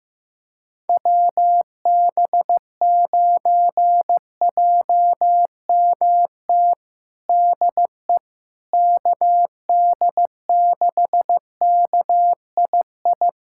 Morse code, heard as a keyed tone, about 15 wpm, 700 Hz. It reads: WB9JMT DE KD6KII
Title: USA Callsign DE Callsign - 6 Characters - 15wpm